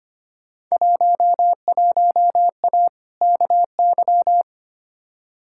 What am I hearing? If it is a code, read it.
Morse code, heard as a keyed tone, about 25 wpm, 700 Hz. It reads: 11A KY
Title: Field Day Exchanges Encore - 25wpm